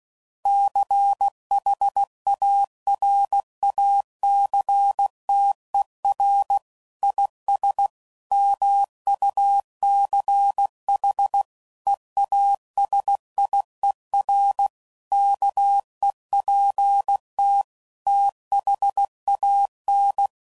Morse code, heard as a keyed tone, about 16 wpm, 800 Hz. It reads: CHARACTER IS MUCH EASIER KEPT THAN
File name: Sayings / Quotes / Character